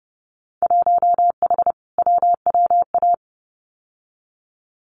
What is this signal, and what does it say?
Morse code, heard as a keyed tone, about 30 wpm, 700 Hz. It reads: 1H WWA
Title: Winter Field Day Exchanges 30wpm